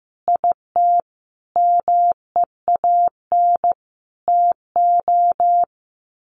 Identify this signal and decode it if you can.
Morse code, heard as a keyed tone, about 15 wpm, 700 Hz. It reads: IT MEAN TO
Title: Sets of 5 Words Encore2 15wpm